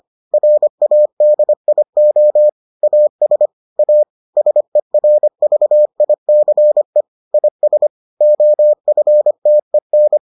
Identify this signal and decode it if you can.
Morse code, heard as a keyed tone, about 25 wpm, 600 Hz. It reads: RADIO AS A SERVICE IS OFTEN